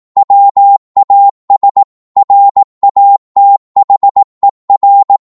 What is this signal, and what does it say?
Morse code, heard as a keyed tone, about 18 wpm, 800 Hz. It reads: WAS RATHER